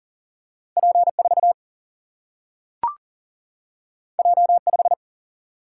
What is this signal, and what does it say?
Morse code, heard as a keyed tone, about 42 wpm, 700 Hz. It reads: P4 J5
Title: Single-Letter Single-Number 40wpm